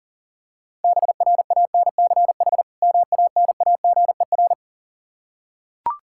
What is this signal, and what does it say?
Morse code, heard as a keyed tone, about 37 wpm, 700 Hz. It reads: BRANCH MANAGER
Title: K1USN SST QSO Practice 40wpm